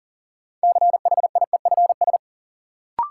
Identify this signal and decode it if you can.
Morse code, heard as a keyed tone, about 40 wpm, 700 Hz. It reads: CHIEFS
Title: Sets of 5 Words Encore - New Word Drill - 40wpm